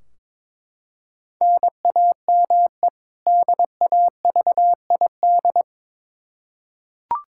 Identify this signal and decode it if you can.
Morse code, heard as a keyed tone, about 22 wpm, 700 Hz. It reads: NAME DAVID
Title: K1USN SST QSO Practice 22wpm